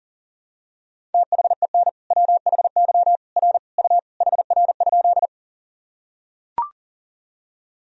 Morse code, heard as a keyed tone, about 40 wpm, 700 Hz. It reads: THEN WHY R U HR?
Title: Sentences from Top 200 Words and Abbreviations 40wpm